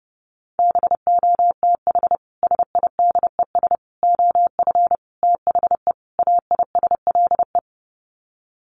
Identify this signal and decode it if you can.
Morse code, heard as a keyed tone, about 30 wpm, 700 Hz. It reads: BOTH SIDES OF THE AISLE
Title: Sets of 5 Words Encore2 30wpm